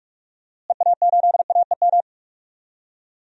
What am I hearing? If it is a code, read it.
Morse code, heard as a keyed tone, about 45 wpm, 700 Hz. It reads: EA8AEM
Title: DXCC-100 Call Signs - 6 Characters - Repeated 3x - 45wpm